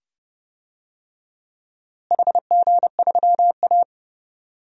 Morse code, heard as a keyed tone, about 29 wpm, 700 Hz. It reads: HG3A
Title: DXCC-100 Call Signs - 4 Characters - Repeated 2x - 30wpm